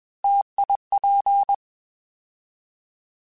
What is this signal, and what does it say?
Morse code, heard as a keyed tone, about 21 wpm, 800 Hz. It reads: TIP
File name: Alphabetical / Alphabetical mp3s / Ts @20